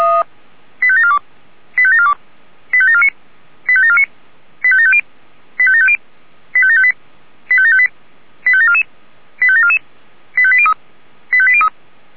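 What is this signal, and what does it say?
Five-tone selective call (ZVEI), heard as quick runs of five tones, tones of about 70 ms. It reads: 85742 85742 85749 85749 85750 85750 85758 85758 85744 85744 85803 85803
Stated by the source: ZVEI five-tone call received on SDR.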